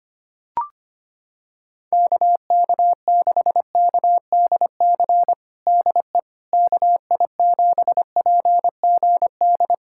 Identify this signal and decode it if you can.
Morse code, heard as a keyed tone, about 25 wpm, 700 Hz. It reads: KK6KDC DE KI7PGD
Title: USA Callsign DE Callsign 25wpm